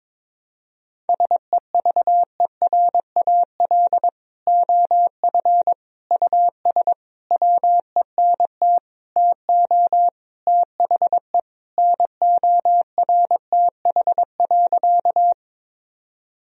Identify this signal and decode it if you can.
Morse code, heard as a keyed tone, about 22 wpm, 700 Hz. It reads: SEVERAL OF US WENT TO THE NORTH.
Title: Sentences from Top 1000 Words - Review Part 2 - 22wpm